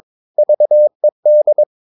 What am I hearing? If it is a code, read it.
Morse code, heard as a keyed tone, about 22 wpm, 600 Hz. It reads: VED
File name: Ferie QRS 22wpm